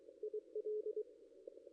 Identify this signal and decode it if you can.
Morse code, heard as a keyed tone, about 22 wpm, 430 Hz. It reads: IL
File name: contests / ss-cw-k5tr-op-n5rz -2200